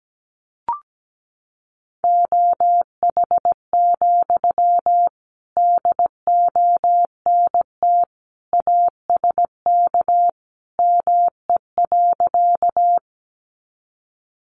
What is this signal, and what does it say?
Morse code, heard as a keyed tone, about 17 wpm, 700 Hz. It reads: OH, DONT ASK ME.
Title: Sentences from Top 500 Words 17wpm